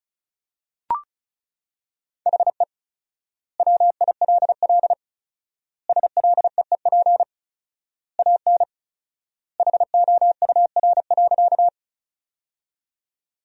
Morse code, heard as a keyed tone, about 34 wpm, 700 Hz. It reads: HE WILL SLEEP AN HOUR.
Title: 4x Word-Spacing - Sentences from Top 500 Words - 35wpm